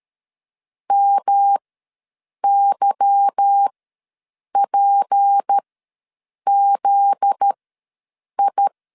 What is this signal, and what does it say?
Morse code, heard as a keyed tone, about 13 wpm, spaced as for 7 wpm, 800 Hz. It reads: MYPZI
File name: LEZIONI / 8 YJ45 / 8 YJ45 ws 12wpm 3146